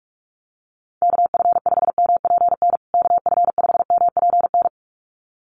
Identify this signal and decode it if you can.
Morse code, heard as a keyed tone, about 44 wpm, 700 Hz. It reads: KF5MPN KF5MPN
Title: USA Call Signs Repeated 2x 45wpm